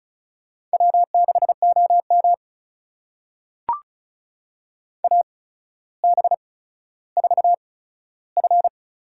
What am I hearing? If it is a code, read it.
Morse code, heard as a keyed tone, about 34 wpm, spaced as for 15 wpm, 700 Hz. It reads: W6OM AB4F
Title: USA Call Signs - 4 Characters - 8x Character Spacing - 35wpm